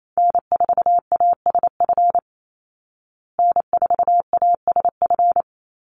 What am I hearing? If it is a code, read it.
Morse code, heard as a keyed tone, about 28 wpm, 700 Hz. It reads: N4ASF N4ASF
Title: Call Signs Repeated 3x - 28wpm